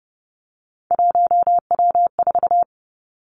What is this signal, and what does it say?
Morse code, heard as a keyed tone, about 31 wpm, 700 Hz. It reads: 1W4
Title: Three Challenging Letter-Number Combinations 30wpm